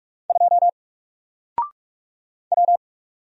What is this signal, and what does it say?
Morse code, heard as a keyed tone, about 44 wpm, 700 Hz. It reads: 2 W
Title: Single Character (Letter:Number:Punctuation:Prosign) 45wpm